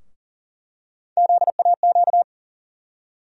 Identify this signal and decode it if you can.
Morse code, heard as a keyed tone, about 40 wpm, 700 Hz. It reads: ZAQ
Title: USA Call Sign Postfixes - 3 Letters 40wpm